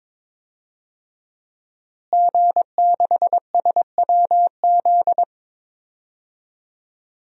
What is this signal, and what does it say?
Morse code, heard as a keyed tone, about 22 wpm, 700 Hz. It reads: G6SWZ
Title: DXCC-100 Call Signs - 5 Characters - Repeated 2x - 22wpm